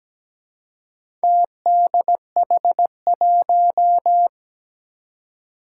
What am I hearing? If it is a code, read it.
Morse code, heard as a keyed tone, about 17 wpm, 700 Hz. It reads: TDH1
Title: Four Challenging Letter-Number Combinations 17wpm